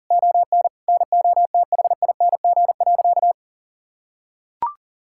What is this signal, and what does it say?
Morse code, heard as a keyed tone, about 37 wpm, 700 Hz. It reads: ON NOTHING.
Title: Sentences from Top 1000 Words - Review Part 5 - 40wpm